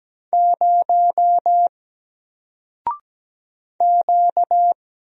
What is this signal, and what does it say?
Morse code, heard as a keyed tone, about 17 wpm, 700 Hz. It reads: 0 Q
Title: Single Letter-Number Alt 17wpm